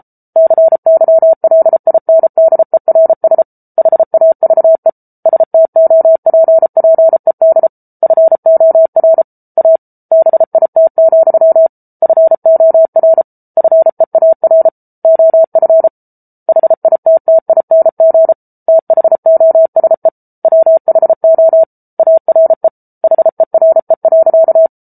Morse code, heard as a keyed tone, about 34 wpm, 650 Hz. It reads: CYLINDERS HAVE STOPPED FOR A BIT, FOR FEAR OF HITTING THOSE WHO ARE HERE.